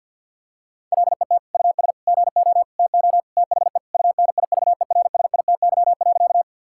Morse code, heard as a keyed tone, about 49 wpm, 700 Hz. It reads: LET US GO TO THE UNIVERSITY.
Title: Sentences from Top 1000 Words - Review Part 5 - 50wpm